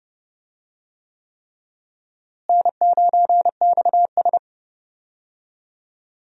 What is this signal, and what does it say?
Morse code, heard as a keyed tone, about 32 wpm, 700 Hz. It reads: N9XS
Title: USA Call Signs - 4 Characters - Repeated 3x - 30wpm